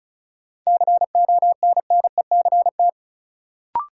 Morse code, heard as a keyed tone, about 36 wpm, 700 Hz. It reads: CONNECT
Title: Sentences from Top 1500 Words - New Word Drill Alt - 35wpm